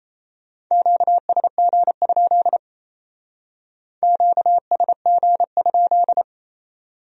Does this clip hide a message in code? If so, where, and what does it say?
Yes: Morse code from 0.7 to 6.2 s, heard as a keyed tone, about 35 wpm, 700 Hz. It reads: QSG? QSG?